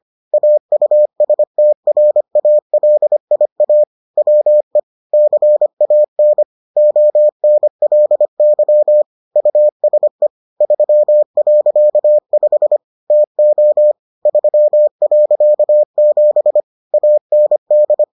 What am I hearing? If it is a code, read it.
Morse code, heard as a keyed tone, about 25 wpm, 600 Hz. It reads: AUSTRALIA WE CAN ONLY USE 3.5 TO 3.7 AND